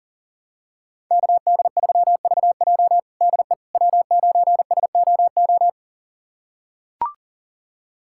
Morse code, heard as a keyed tone, about 40 wpm, 700 Hz. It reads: KD3VJ DE W9SOO
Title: USA Callsign DE Callsign - 5 Characters - 40wpm